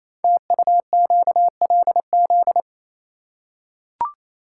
Morse code, heard as a keyed tone, about 29 wpm, 700 Hz. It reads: TUQLZ
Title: Code Groups - 5 Letters - 28wpm